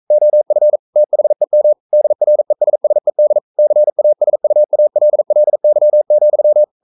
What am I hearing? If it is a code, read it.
Morse code, heard as a keyed tone, about 42 wpm, 600 Hz. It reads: OF THEM DRESSED CASUALLY,
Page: Romanen Never af Ken Follet på engelsk.